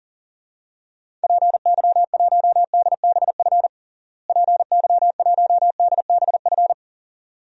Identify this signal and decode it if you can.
Morse code, heard as a keyed tone, about 40 wpm, 700 Hz. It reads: PY1DBF PY1DBF
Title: DXCC-100 Call Signs - 6 Characters - Repeated 2x - 40wpm